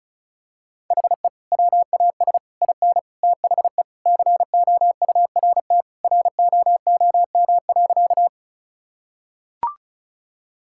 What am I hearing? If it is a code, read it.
Morse code, heard as a keyed tone, about 34 wpm, 700 Hz. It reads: HE WAS IN THE COURT ROOM.
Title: Sentences from Top 1600 Words 35wpm